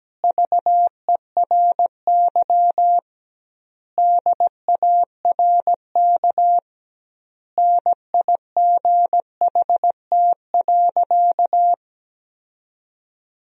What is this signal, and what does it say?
Morse code, heard as a keyed tone, about 17 wpm, 700 Hz. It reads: VERY DARK NIGHT.
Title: 2x Word-Spacing - Sentences from Top 500 Words - 17wpm